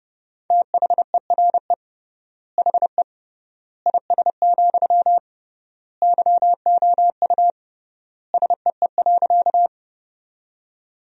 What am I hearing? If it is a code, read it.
Morse code, heard as a keyed tone, about 30 wpm, 700 Hz. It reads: THERE HE IS, YOU SEE.
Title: 3x Word-Spacing - Sentences from Top 100 Words - 30wpm